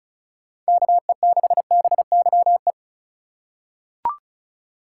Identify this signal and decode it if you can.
Morse code, heard as a keyed tone, about 36 wpm, 700 Hz. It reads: KE6BYE
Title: USA Call Signs - 6 Characters - 4x Character Spacing - 35wpm